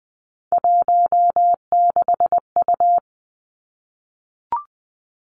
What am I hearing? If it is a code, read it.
Morse code, heard as a keyed tone, about 20 wpm, 700 Hz. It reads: 16U
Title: Three Challenging Letter-Number Combinations 20wpm